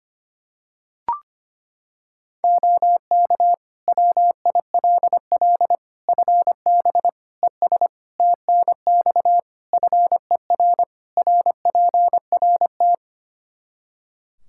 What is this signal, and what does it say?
Morse code, heard as a keyed tone, about 25 wpm, 700 Hz. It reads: OK WILL FB ES TNX FER RPRT
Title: Practice QSO - Exchange 2 - Info - 25wpm